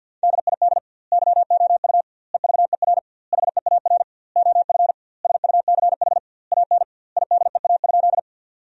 Morse code, heard as a keyed tone, about 50 wpm, 700 Hz. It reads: DID YOU EVER HEAR OF SUCH AN IDEA?
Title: 2x Word-Spacing - Sentences from Top 400 Words - 50wpm